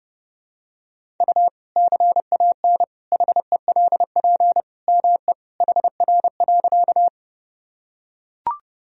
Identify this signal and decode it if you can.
Morse code, heard as a keyed tone, about 32 wpm, 700 Hz. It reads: U CAN HELP ME HR.
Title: Sentences from Top 200 Words and Abbreviations 30wpm